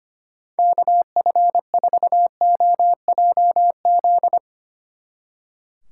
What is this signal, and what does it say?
Morse code, heard as a keyed tone, about 31 wpm, 700 Hz. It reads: KF4OJZ
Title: Speed Racing - USA Call Signs 25wpm